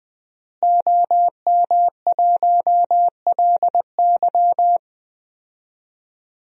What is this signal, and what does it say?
Morse code, heard as a keyed tone, about 20 wpm, 700 Hz. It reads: OM1LY
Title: DXCC-100 Call Signs - 5 Characters - Repeated 3x - 20wpm